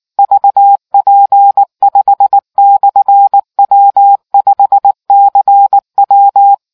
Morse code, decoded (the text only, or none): VP5/W5CW